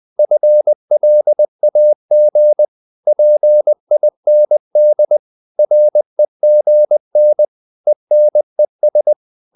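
Morse code, heard as a keyed tone, about 20 wpm, 600 Hz. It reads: FLAG PIND REGN ENES
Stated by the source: Morse code practice of short Danish words at 20 wpm, from CW med Gnister.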